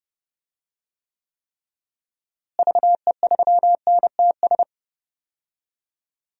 Morse code, heard as a keyed tone, about 31 wpm, 700 Hz. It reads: VE3NTS
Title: DXCC-100 Call Signs - 6 Characters - Repeated 3x - 30wpm